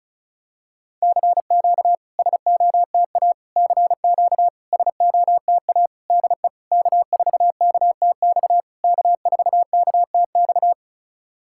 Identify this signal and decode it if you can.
Morse code, heard as a keyed tone, about 36 wpm, 700 Hz. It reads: CQ SOTA CQ SOTA DE K4KTX K4KTX
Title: SOTA QSO Practice 35wpm